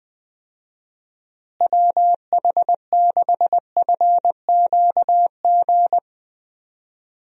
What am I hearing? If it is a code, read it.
Morse code, heard as a keyed tone, about 20 wpm, 700 Hz. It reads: WH6FQG
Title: USA Call Signs - 6 Characters - Repeated 3x - 20wpm